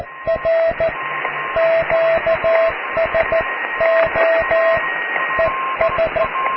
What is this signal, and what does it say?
Morse code, heard as a keyed tone, about 14 wpm, 660 Hz, with radio noise. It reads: R QSO ES
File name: cw / QSO+QRM